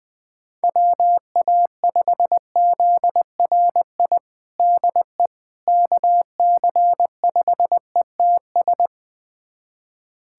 Morse code, heard as a keyed tone, about 21 wpm, 700 Hz. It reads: WA5ZRI DE KC5ETS
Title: USA Callsign DE Callsign 20wpm